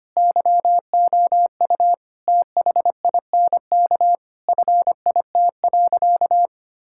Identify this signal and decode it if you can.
Morse code, heard as a keyed tone, about 25 wpm, 700 Hz. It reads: YOU THINK FIT.
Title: Sentences from Top 1000 Words - Review Part 5 - 25wpm